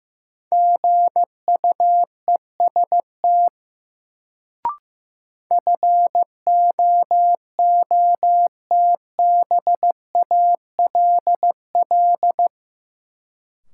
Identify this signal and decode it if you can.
Morse code, heard as a keyed tone, about 15 wpm, 700 Hz. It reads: GUEST FOOTBALL
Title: Sentences from Top 1600 Words - New Word Drill Alt - 15wpm